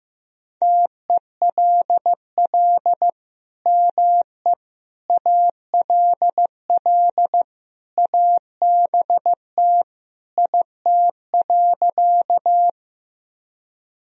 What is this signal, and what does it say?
Morse code, heard as a keyed tone, about 15 wpm, 700 Hz. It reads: TELL ME ALL ABT IT.
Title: Sentences from Top 200 Words and Abbreviations 15wpm